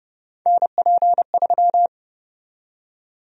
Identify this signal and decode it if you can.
Morse code, heard as a keyed tone, about 30 wpm, 700 Hz. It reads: NP3
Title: USA Call Sign Prefixes - Repeated 2x - 30wpm